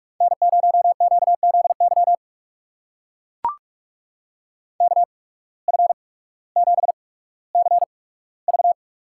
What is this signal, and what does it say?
Morse code, heard as a keyed tone, about 44 wpm, spaced as for 19 wpm, 700 Hz. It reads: N0QZY KF7CV
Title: USA Call Signs - 5 Characters - 8x Character Spacing - 45wpm